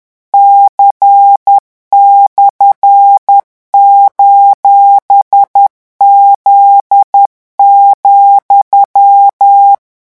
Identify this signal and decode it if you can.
Morse code, heard as a keyed tone, about 10 wpm, 800 Hz. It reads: C/:Z,
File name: morse mp3 / revision17-vitesse 10 mots